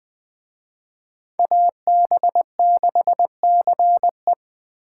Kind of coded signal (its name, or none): Morse code